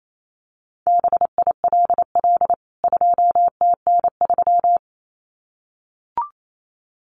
Morse code, heard as a keyed tone, about 28 wpm, 700 Hz. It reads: BILL 2TN3
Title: CWT Exchange 28wpm